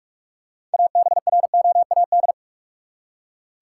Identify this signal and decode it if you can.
Morse code, heard as a keyed tone, about 43 wpm, 700 Hz. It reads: ABROAD
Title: Six Letter Words - 45wpm